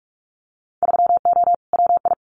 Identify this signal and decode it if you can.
Morse code, heard as a keyed tone, about 44 wpm, 700 Hz. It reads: 3O WI
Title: Winter Field Day Exchanges 45wpm